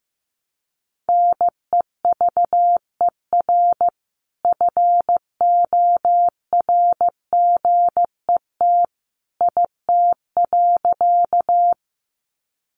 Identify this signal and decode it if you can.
Morse code, heard as a keyed tone, about 15 wpm, 700 Hz. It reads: NEVER FORGET IT.
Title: Sentences from Top 1200 Words 15wpm